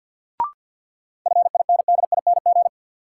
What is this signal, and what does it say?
Morse code, heard as a keyed tone, about 53 wpm, 700 Hz. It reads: FINDING
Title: Sentences from Top 1100 Words - New Word Drill Alt - 50wpm